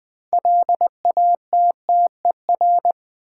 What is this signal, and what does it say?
Morse code, heard as a keyed tone, about 20 wpm, 700 Hz. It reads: LATTER
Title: Sentences from Top 1600 Words - New Word Drill - 20wpm